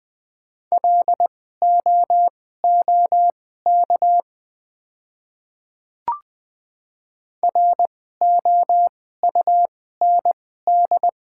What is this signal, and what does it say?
Morse code, heard as a keyed tone, about 20 wpm, spaced as for 14 wpm, 700 Hz. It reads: LOOK ROUND
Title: 2x Character Spacing - Top 300 Words - 20wpm